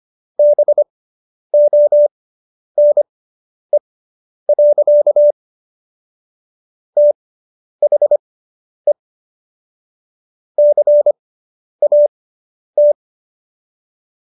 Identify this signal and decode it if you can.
Morse code, heard as a keyed tone, about 25 wpm, spaced as for 10 wpm, 590 Hz. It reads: BONE. THE CAT